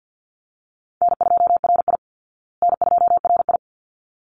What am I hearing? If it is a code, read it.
Morse code, heard as a keyed tone, about 50 wpm, 700 Hz. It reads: N2RI N2RI
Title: Easy Call Signs Repeated 2x - 50wpm